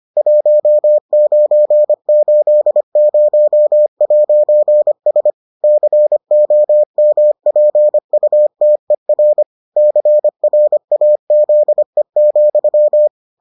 Morse code, heard as a keyed tone, about 25 wpm, 600 Hz. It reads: 1980'S COMPUTER CRAZE,